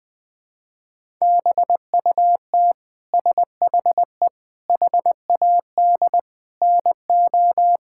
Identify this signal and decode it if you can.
Morse code, heard as a keyed tone, about 20 wpm, 700 Hz. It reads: BUT SHE HAD NO
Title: Sentences from Top 1600 Words 20wpm